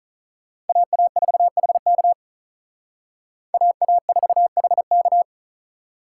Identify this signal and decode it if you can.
Morse code, heard as a keyed tone, about 43 wpm, 700 Hz. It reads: AA4HK AA4HK
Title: Speed Racing - USA Call Signs 35wpm